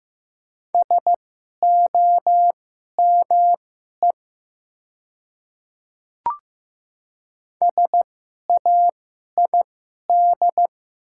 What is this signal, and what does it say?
Morse code, heard as a keyed tone, about 15 wpm, spaced as for 11 wpm, 700 Hz. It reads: SOME SAID
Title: 2x Character Spacing - Top 100 Words - 15wpm